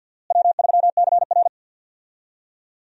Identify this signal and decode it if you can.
Morse code, heard as a keyed tone, about 49 wpm, 690 Hz. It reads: W3CR
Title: USA Call Signs - 4 Characters - Repeated 2x - 50wpm